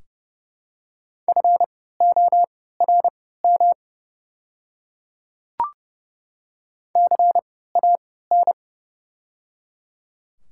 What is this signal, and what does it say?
Morse code, heard as a keyed tone, about 30 wpm, spaced as for 17 wpm, 700 Hz. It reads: FORM CAN
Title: 3x Character Spacing - Top 200 Words - 30wpm